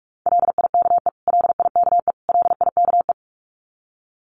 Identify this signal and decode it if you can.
Morse code, heard as a keyed tone, about 44 wpm, 700 Hz. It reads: LIKE LIKE LIKE
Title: Top 500 Words Repeated 45wpm